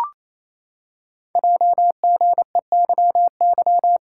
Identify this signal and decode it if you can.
Morse code, heard as a keyed tone, about 29 wpm, 700 Hz. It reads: JGEYY
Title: Code Groups - 5 Letters - 28wpm